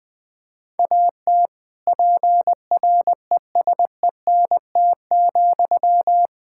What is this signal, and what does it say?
Morse code, heard as a keyed tone, about 20 wpm, 700 Hz. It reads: AT PRESENT,
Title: Sentences from Top 1000 Words - Review Part 5 - 20wpm